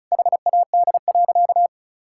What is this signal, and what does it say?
Morse code, heard as a keyed tone, about 36 wpm, 700 Hz. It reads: HAD.
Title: Sentences from Top 1000 Words - Review Part 6 - 35wpm